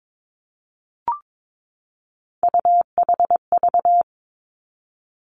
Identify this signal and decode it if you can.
Morse code, heard as a keyed tone, about 22 wpm, 700 Hz. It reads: UHV
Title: Three Challenging Letter-Number Combinations 22wpm